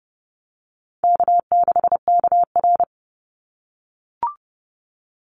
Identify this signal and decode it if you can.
Morse code, heard as a keyed tone, about 30 wpm, 700 Hz. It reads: K6KR
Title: Easy Call Signs - 30wpm